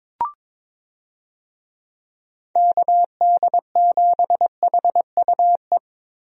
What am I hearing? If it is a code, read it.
Morse code, heard as a keyed tone, about 22 wpm, 700 Hz. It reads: KD7HUE
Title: USA Call Signs - 6 Characters - Repeated 3x - 22wpm